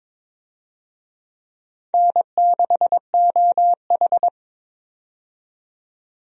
Morse code, heard as a keyed tone, about 21 wpm, 700 Hz. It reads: N6OH
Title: USA Call Signs - 4 Characters - 22wpm